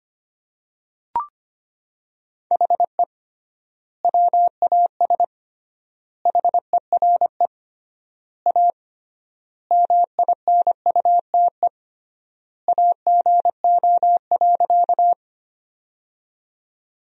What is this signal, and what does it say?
Morse code, heard as a keyed tone, about 25 wpm, 700 Hz. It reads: HE WAS HERE A MINUTE AGO.
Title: 3x Word-Spacing - Sentences from Top 500 Words - 25wpm